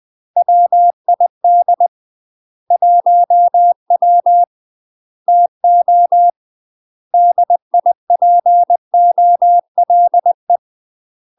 Morse code, heard as a keyed tone, about 20 wpm, 700 Hz. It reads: WID 1W TO DIPOLE
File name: 20WPM